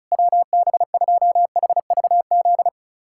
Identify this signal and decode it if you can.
Morse code, heard as a keyed tone, about 36 wpm, 700 Hz. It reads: WB2HVZ
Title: USA Callsign DE Callsign 35wpm